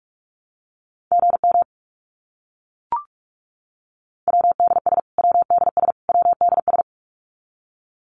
Morse code, heard as a keyed tone, about 44 wpm, 700 Hz. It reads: GM WDS WDS WDS
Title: Abbreviations Repeated 3x - 45wpm